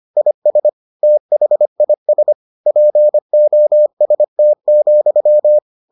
Morse code, heard as a keyed tone, about 25 wpm, 600 Hz. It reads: IS THIS POST,